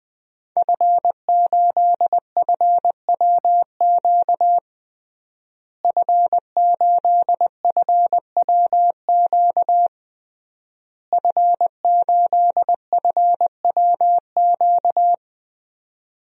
Morse code, heard as a keyed tone, about 21 wpm, 700 Hz. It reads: F8FWQ F8FWQ F8FWQ
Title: DXCC-100 Call Signs - 5 Characters - Repeated 3x - 20wpm